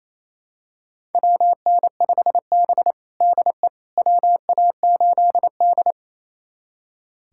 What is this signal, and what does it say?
Morse code, heard as a keyed tone, about 28 wpm, 700 Hz. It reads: WN5B DE WA8D
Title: USA Callsign DE Callsign - 4 Characters - 28wpm